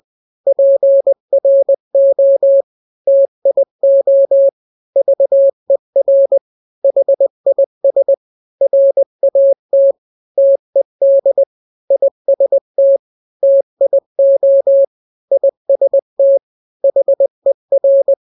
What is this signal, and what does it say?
Morse code, heard as a keyed tone, about 20 wpm, 550 Hz. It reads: PRO TIO VER HIS RAT TED IST TIO IST HER
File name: n-grams / 3-grams 20 18 WPM